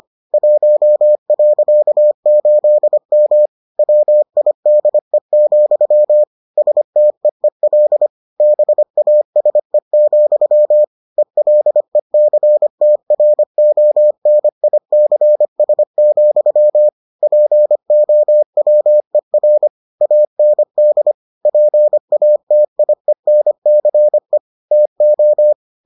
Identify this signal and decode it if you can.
Morse code, heard as a keyed tone, about 25 wpm, 600 Hz. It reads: 1.8M WIDE, STEEL BASE, ELECTRONICS, POWER AND PATIENCE TO